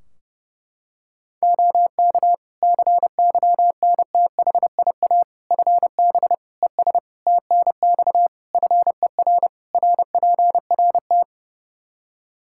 Morse code, heard as a keyed tone, about 30 wpm, 700 Hz. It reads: OK CYNTHIA FB ES TNX FER RPRT
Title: K1USN SST QSO Practice 30wpm